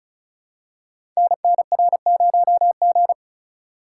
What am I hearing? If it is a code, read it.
Morse code, heard as a keyed tone, about 36 wpm, 700 Hz. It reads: NNR0G
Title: Code Groups - 5 Letter:Numbers - 35wpm